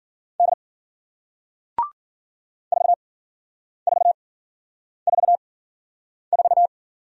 Morse code, heard as a keyed tone, about 55 wpm, spaced as for 9 wpm, 700 Hz. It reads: D 4444